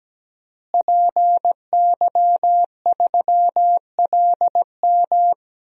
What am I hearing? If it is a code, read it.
Morse code, heard as a keyed tone, about 17 wpm, 700 Hz. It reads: PY3LM
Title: DXCC-100 Call Signs - 5 Characters - Repeated 3x - 17wpm